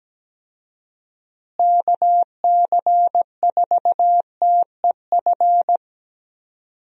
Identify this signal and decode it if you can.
Morse code, heard as a keyed tone, about 17 wpm, 700 Hz. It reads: KC4TEF
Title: USA Call Signs - 6 Characters - Repeated 2x - 17wpm